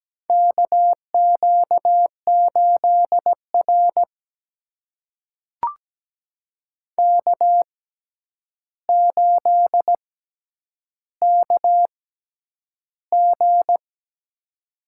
Morse code, heard as a keyed tone, about 17 wpm, spaced as for 9 wpm, 700 Hz. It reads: KQ8R K8KG
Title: USA Call Signs - 4 Characters - 6x Character Spacing - 17wpm